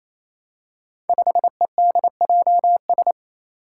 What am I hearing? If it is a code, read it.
Morse code, heard as a keyed tone, about 27 wpm, 700 Hz. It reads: 5EDJS
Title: Five Challenging Letter-Number Combinations 28wpm